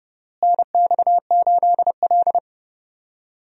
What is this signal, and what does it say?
Morse code, heard as a keyed tone, about 32 wpm, 700 Hz. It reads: NX8L
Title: USA Call Signs - 4 Characters - Repeated 2x - 30wpm